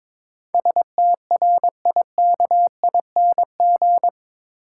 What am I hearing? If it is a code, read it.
Morse code, heard as a keyed tone, about 22 wpm, 700 Hz. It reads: STRIKING
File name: practice / 22wpm / Eight Letter Words - 22wpm